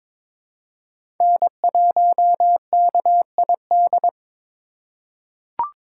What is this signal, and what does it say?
Morse code, heard as a keyed tone, about 22 wpm, 700 Hz. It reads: N1KID
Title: USA Call Signs - 3x Character Spacing - 22wpm